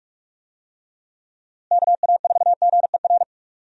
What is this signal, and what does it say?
Morse code, heard as a keyed tone, about 41 wpm, 700 Hz. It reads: KA4GER
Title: USA Call Signs - 6 Characters - Repeated 2x - 45wpm